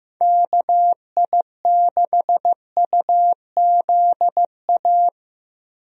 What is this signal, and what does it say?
Morse code, heard as a keyed tone, about 15 wpm, 700 Hz. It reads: KI6UZA
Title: USA Call Signs - 6 Characters - Repeated 2x - 15wpm